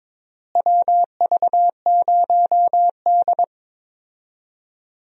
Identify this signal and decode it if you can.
Morse code, heard as a keyed tone, about 22 wpm, 700 Hz. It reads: WV0D
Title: USA Call Signs - 4 Characters - Repeated 3x - 22wpm